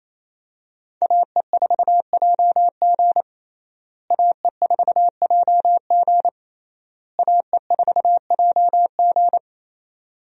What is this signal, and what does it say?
Morse code, heard as a keyed tone, about 28 wpm, 700 Hz. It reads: AE4JG AE4JG AE4JG
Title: USA Call Signs - 5 Characters - Repeated 3x - 28wpm